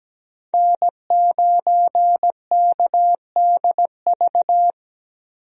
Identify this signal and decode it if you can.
Morse code, heard as a keyed tone, about 17 wpm, 700 Hz. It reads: N9KDV
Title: USA Call Signs - 5 Characters - Repeated 2x - 17wpm